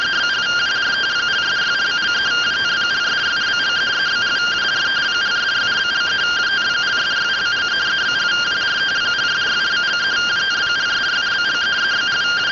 Two-tone teleprinter signal, about 170 Hz shift.